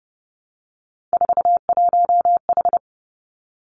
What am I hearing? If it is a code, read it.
Morse code, heard as a keyed tone, about 32 wpm, 700 Hz. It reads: 41H
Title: Three Challenging Letter-Number Combinations 30wpm